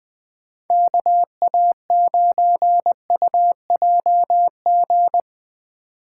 Morse code, heard as a keyed tone, about 20 wpm, 700 Hz. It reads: KA9UJG
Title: USA Call Signs - 6 Characters - 6x Character Spacing - 20wpm